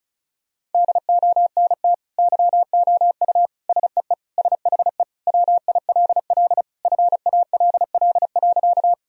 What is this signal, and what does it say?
Morse code, heard as a keyed tone, about 36 wpm, 700 Hz. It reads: DONT YOU SEE SHE WILL FALL.
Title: Sentences from Top 1000 Words - Review Part 2 - 35wpm